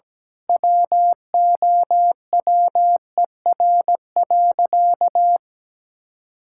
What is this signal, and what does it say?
Morse code, heard as a keyed tone, about 17 wpm, 700 Hz. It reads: WOWER.
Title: Pangram Challenge v2 - 17wpm